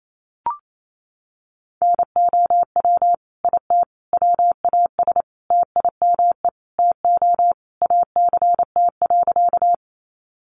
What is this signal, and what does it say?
Morse code, heard as a keyed tone, about 28 wpm, 700 Hz. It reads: NOW IT WAS TIME TO ACT.
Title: Sentences from Top 200 Words 28wpm